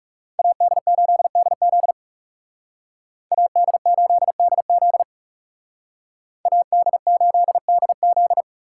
Morse code, heard as a keyed tone, about 38 wpm, 700 Hz. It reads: AD8DZ AD8DZ AD8DZ